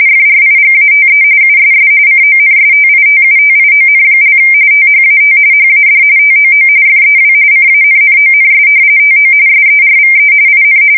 RTTY, about 45.45 baud. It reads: FRIGATE USS CONSTITUTION) THAT HAD TOO FEW CANNONS TO SERVE IN
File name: fsk170 / battleshipnj